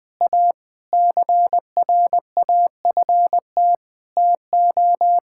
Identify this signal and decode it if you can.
Morse code, heard as a keyed tone, about 20 wpm, 700 Hz. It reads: A CRAFT TO
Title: Sentences from Top 1000 Words - Review Part 6 - 20wpm